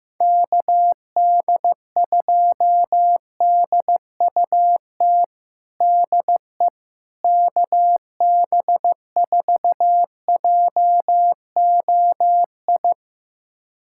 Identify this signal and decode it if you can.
Morse code, heard as a keyed tone, about 15 wpm, 700 Hz. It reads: KD2DUT DE KB4JOI
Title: USA Callsign DE Callsign - 6 Characters - 15wpm